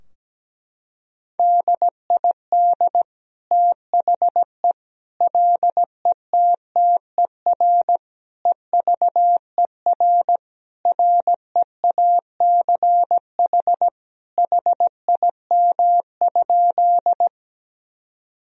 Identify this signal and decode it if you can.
Morse code, heard as a keyed tone, about 17 wpm, 700 Hz. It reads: DID THE LETTER EVER REACH HIM?
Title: Sentences from Top 1000 Words - Review Part 2 - 17wpm